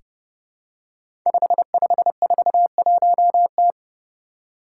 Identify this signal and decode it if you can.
Morse code, heard as a keyed tone, about 30 wpm, 700 Hz. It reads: 5541T
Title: Five Challenging Letter-Number Combinations 30wpm